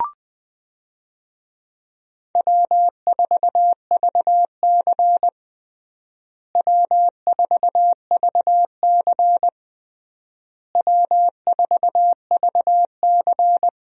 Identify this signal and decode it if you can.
Morse code, heard as a keyed tone, about 20 wpm, 700 Hz. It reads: W4VC W4VC W4VC
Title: USA Call Signs - 4 Characters - Repeated 3x - 20wpm